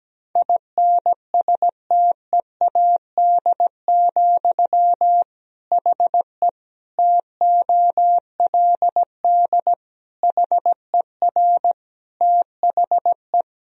Morse code, heard as a keyed tone, about 17 wpm, 700 Hz. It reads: INSTEAD, HE TOLD HER THE
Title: Sentences from Top 1000 Words - Review Part 4 - 17wpm